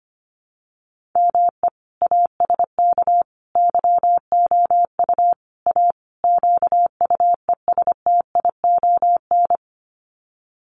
Morse code, heard as a keyed tone, about 25 wpm, 700 Hz. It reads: ME ASK YOU A QUESTION
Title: Sets of 5 Words Encore2 25wpm